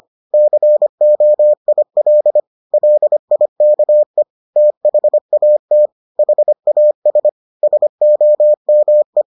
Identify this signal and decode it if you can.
Morse code, heard as a keyed tone, about 25 wpm, 600 Hz. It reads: COIL LIKE THAT HAS SOME